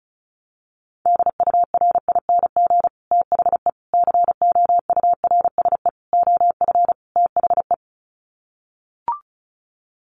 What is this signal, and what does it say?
Morse code, heard as a keyed tone, about 34 wpm, 700 Hz. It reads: DURING THE COURSE OF THE
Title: Sets of 5 Words Encore 35wpm